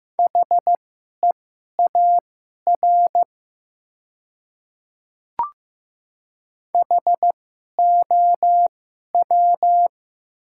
Morse code, heard as a keyed tone, about 15 wpm, spaced as for 11 wpm, 700 Hz. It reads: HEAR HOW
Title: 2x Character Spacing - Top 300 Words - 15wpm